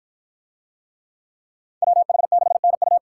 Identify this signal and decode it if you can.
Morse code, heard as a keyed tone, about 53 wpm, 700 Hz. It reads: WH6NU